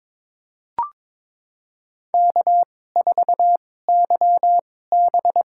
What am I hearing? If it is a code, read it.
Morse code, heard as a keyed tone, about 22 wpm, spaced as for 16 wpm, 700 Hz. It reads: K4YB
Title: USA Call Signs - 4 Characters - 2x Character Spacing - 22wpm